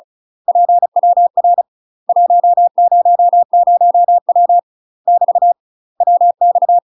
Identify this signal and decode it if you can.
Morse code, heard as a keyed tone, about 34 wpm, 700 Hz. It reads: PWR 100W <BT> WX
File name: UK MASTER QSOs TRAINING / 35WPM / UK MASTER QSOs 0043